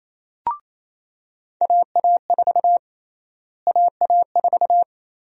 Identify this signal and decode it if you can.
Morse code, heard as a keyed tone, about 27 wpm, 700 Hz. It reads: AA4 AA4
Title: USA Call Sign Prefixes - Repeated 2x - 28wpm